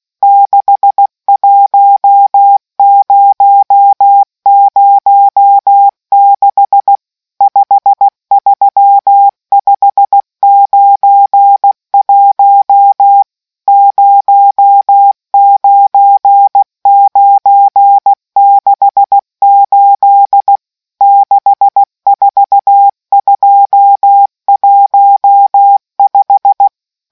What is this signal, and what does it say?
Morse code, heard as a keyed tone, about 16 wpm, 800 Hz. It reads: 61006 53591 09968 64215